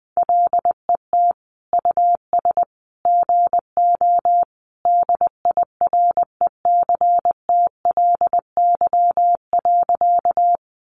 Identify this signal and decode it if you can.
Morse code, heard as a keyed tone, about 20 wpm, 700 Hz. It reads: LET US GO DIRECTLY.
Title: Sentences from Top 800 Words 20wpm